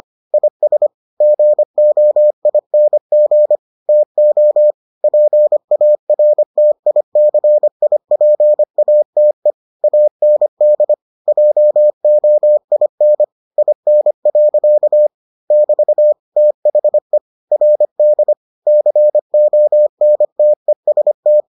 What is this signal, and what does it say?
Morse code, heard as a keyed tone, about 25 wpm, 600 Hz. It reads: IS GOING TO PARTICIPATE AND JOIN IN. <BT> THE RD CONTEST